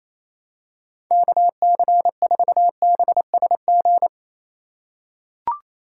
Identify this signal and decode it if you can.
Morse code, heard as a keyed tone, about 27 wpm, 700 Hz. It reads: KC4BSG
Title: USA Call Signs - 6 Characters - 4x Character Spacing - 28wpm